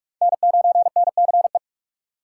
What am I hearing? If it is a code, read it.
Morse code, heard as a keyed tone, about 55 wpm, 700 Hz. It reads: N9NCE
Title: Speed Racing - USA 1x3 Call Signs 45wpm